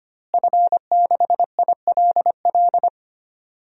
Morse code, heard as a keyed tone, about 25 wpm, 700 Hz. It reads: F6ILL
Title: DXCC-100 Call Signs - 5 Characters - Repeated 2x - 25wpm